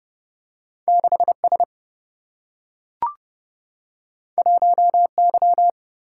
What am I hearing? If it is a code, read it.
Morse code, heard as a keyed tone, about 29 wpm, 700 Hz. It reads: 6 S 1 Y
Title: Single-Letter Single-Number 30wpm